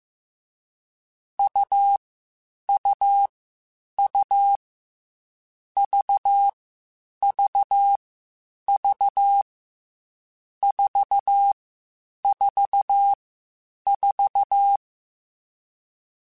Morse code, heard as a keyed tone, about 15 wpm, 800 Hz. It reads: U U U V V V 4 4 4
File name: Short Runs / Short Runs-mp3s / AUV AUV4 @13